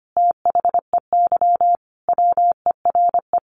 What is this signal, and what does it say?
Morse code, heard as a keyed tone, about 25 wpm, 700 Hz. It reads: THEY WERE
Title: Sentences from Top 1300 Words 25wpm